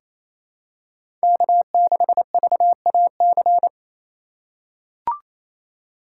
Morse code, heard as a keyed tone, about 28 wpm, 700 Hz. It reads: K6VAC
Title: USA Call Signs - 5 Characters - 4x Character Spacing - 28wpm